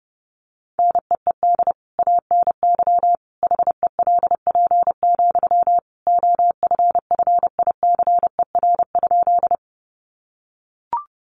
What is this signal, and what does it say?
Morse code, heard as a keyed tone, about 30 wpm, 700 Hz. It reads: NEED ANY HELP, OFFICER?
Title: Sentences from Top 1200 Words 30wpm